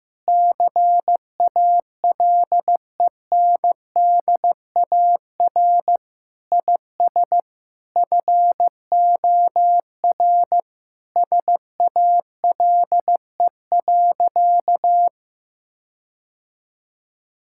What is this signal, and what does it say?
Morse code, heard as a keyed tone, about 15 wpm, 700 Hz. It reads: CALENDAR IS FOR SALE.
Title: Sentences from Top 1600 Words 15wpm